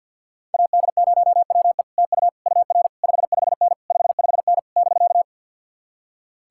Morse code, heard as a keyed tone, about 62 wpm, 700 Hz. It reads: AD0WE TU UR 55N 55N <BK>
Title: Speed-Racing Common QSO Phrases - 50wpm